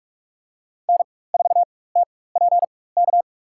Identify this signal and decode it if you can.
Morse code, heard as a keyed tone, about 45 wpm, spaced as for 21 wpm, 690 Hz. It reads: N4TPK
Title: USA Call Signs - 4x Character Spacing - 45wpm